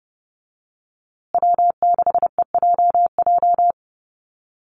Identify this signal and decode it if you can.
Morse code, heard as a keyed tone, about 30 wpm, 700 Hz. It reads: W6EJJ
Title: USA Call Signs Repeated 3x 30wpm